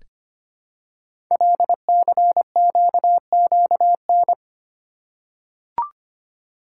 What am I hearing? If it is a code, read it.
Morse code, heard as a keyed tone, about 25 wpm, 700 Hz. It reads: LCQQN
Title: Code Groups - 5 Letters - 25wpm